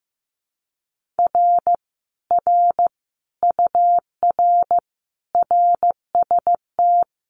Morse code, heard as a keyed tone, about 15 wpm, 700 Hz. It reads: R R UR RST
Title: Common QSO Phrases Encore 15wpm